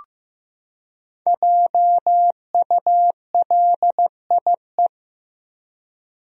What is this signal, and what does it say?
Morse code, heard as a keyed tone, about 15 wpm, 700 Hz. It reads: JULIE
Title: USA Common Names - Max 5 Letters - 15wpm